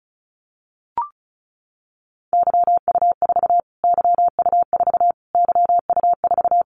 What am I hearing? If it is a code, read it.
Morse code, heard as a keyed tone, about 34 wpm, 700 Hz. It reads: YU4 YU4 YU4
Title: Callsign Prefixes for Top 25 DXCC - Repeated 3x - 35wpm